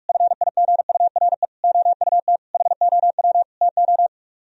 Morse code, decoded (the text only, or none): FIGURE OUT HOW TO